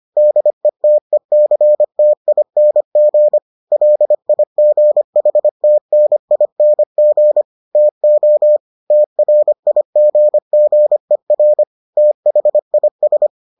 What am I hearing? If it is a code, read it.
Morse code, heard as a keyed tone, about 25 wpm, 600 Hz. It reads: DETECTING LIGHTNING TO TRIGGER THIS